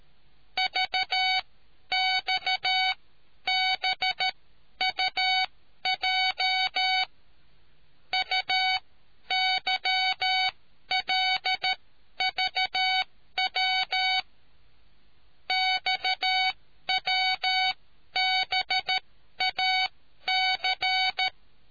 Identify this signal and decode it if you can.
Morse code, heard as a keyed tone, about 13 wpm, spaced as for 11 wpm, 750 Hz. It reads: VXBUJ UYLVW XWBAC